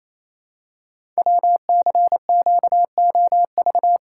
Morse code, heard as a keyed tone, about 27 wpm, 700 Hz. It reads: WCQOV
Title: Code Groups - 5 Letters - 28wpm